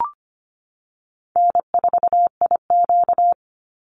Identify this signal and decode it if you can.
Morse code, heard as a keyed tone, about 25 wpm, 700 Hz. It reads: N4IQ
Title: Call Signs Repeated 2x - 25wpm